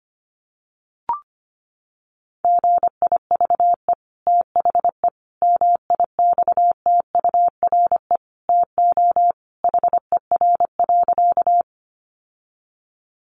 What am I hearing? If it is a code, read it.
Morse code, heard as a keyed tone, about 25 wpm, 700 Hz. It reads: GIVE THE MIXTURE TO HER.
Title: Sentences from Top 1500 Words 25wpm